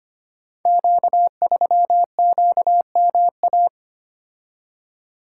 Morse code, heard as a keyed tone, about 25 wpm, 700 Hz. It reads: Q3QMA
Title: Code Groups - 5 Letter:Numbers - 25wpm